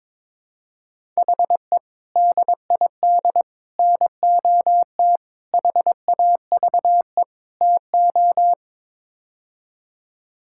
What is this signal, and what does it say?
Morse code, heard as a keyed tone, about 22 wpm, 700 Hz. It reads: HE DID NOT HAVE TO
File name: Build A Phrase - 5-Word Phrases - 22wpm